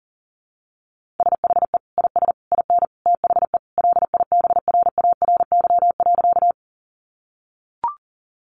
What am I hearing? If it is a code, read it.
Morse code, heard as a keyed tone, about 40 wpm, 700 Hz. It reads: SHE IS IN THE LIBRARY.
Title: Sentences from Top 600 Words 40wpm